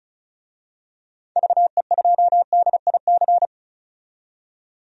Morse code, heard as a keyed tone, about 36 wpm, 700 Hz. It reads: VE2DIC
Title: DXCC-100 Call Signs - 6 Characters - 35wpm